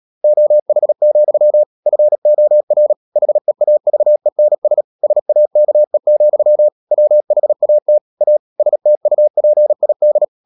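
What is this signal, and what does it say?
Morse code, heard as a keyed tone, about 37 wpm, 600 Hz. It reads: OH, FOR HEAVENS SAKE, WHAT A STUPID
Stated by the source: Morse practice at 36 wpm, text of Ken Follett's novel Never, from CW med Gnister.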